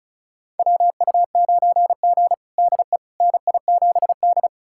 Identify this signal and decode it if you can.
Morse code, heard as a keyed tone, about 34 wpm, 700 Hz. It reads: WU9G DE NI7D
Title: USA Callsign DE Callsign - 4 Characters - 35wpm